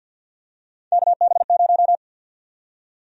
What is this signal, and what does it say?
Morse code, heard as a keyed tone, about 47 wpm, 700 Hz. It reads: KB0
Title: USA Call Sign Prefixes - 50wpm